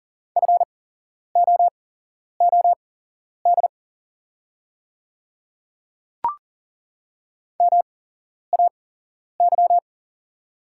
Morse code, heard as a keyed tone, about 40 wpm, spaced as for 11 wpm, 690 Hz. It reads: FOOD MAY
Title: 8x Character Spacing - Top 300 Words - 40wpm